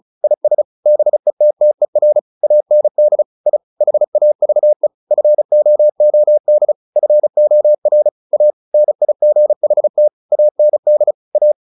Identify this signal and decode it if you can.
Morse code, heard as a keyed tone, about 34 wpm, 600 Hz. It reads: IS BETTER AND I HAVE FOOD FOR A NIGHT AND A
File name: the old man and the sea 2 35wpm